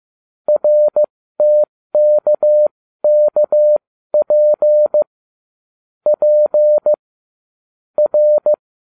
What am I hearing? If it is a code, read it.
Morse code, heard as a keyed tone, about 15 wpm, spaced as for 13 wpm, 600 Hz. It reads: RTKKP P R
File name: cw / ubacw07 1